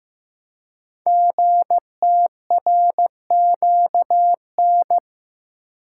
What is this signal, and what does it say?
Morse code, heard as a keyed tone, about 15 wpm, 700 Hz. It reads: GTRQN
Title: Code Groups - 5 Letters - 15wpm